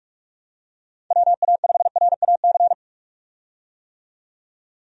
Morse code, heard as a keyed tone, about 46 wpm, 700 Hz. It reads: WA5RAC
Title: USA Call Signs - 6 Characters - 45wpm